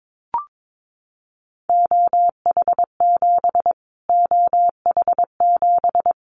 Morse code, heard as a keyed tone, about 22 wpm, 700 Hz. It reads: OH7 OH7
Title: Callsign Prefixes for Top 25 DXCC - Repeated 3x - 22wpm